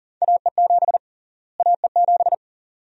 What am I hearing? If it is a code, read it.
Morse code, heard as a keyed tone, about 40 wpm, 700 Hz. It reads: AE7 AE7
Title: USA Call Sign Prefixes - Repeated 2x - 40wpm